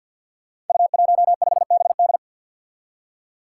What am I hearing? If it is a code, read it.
Morse code, heard as a keyed tone, about 47 wpm, 700 Hz. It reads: U15BD
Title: Five Challenging Letter-Number Combinations 50wpm